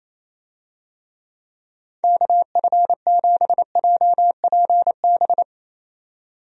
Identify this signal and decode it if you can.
Morse code, heard as a keyed tone, about 27 wpm, 700 Hz. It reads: KF7JPB
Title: USA Call Signs - 6 Characters - 28wpm